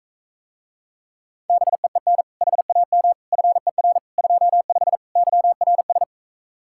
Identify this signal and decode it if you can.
Morse code, heard as a keyed tone, about 45 wpm, 700 Hz. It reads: BEEN HAM FER 25 YRS